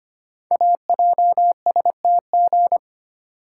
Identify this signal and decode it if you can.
Morse code, heard as a keyed tone, about 25 wpm, 700 Hz. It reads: AJSTG
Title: Code Groups - 5 Letters - 25wpm